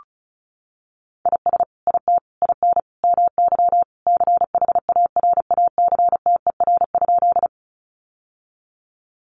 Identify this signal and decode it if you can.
Morse code, heard as a keyed tone, about 34 wpm, 700 Hz. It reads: IS IT IN MY CHARACTER?
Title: Sentences from Top 1200 Words 35wpm